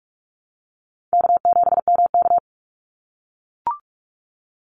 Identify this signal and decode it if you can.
Morse code, heard as a keyed tone, about 44 wpm, 700 Hz. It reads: K7MK
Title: NA SOTA Activator Callsigns 45wpm